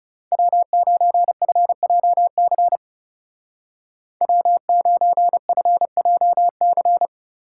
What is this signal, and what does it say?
Morse code, heard as a keyed tone, about 37 wpm, 700 Hz. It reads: W9FJC W9FJC
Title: Speed Racing - USA 1x3 Call Signs 30wpm